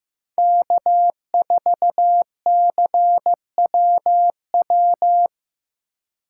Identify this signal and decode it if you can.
Morse code, heard as a keyed tone, about 16 wpm, 700 Hz. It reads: K4CWW
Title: USA Call Signs - 5 Characters - 4x Character Spacing - 15wpm